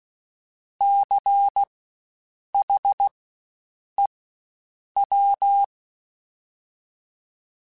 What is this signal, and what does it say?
Morse code, heard as a keyed tone, about 16 wpm, spaced as for 7 wpm, 800 Hz. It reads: CHEW
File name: Spaced Alphas / Spaced Alphas mp3s / Cs sp @15